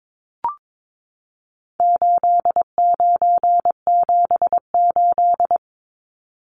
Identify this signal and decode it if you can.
Morse code, heard as a keyed tone, about 22 wpm, 700 Hz. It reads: 8978
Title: Numbers - 4-Digit 22wpm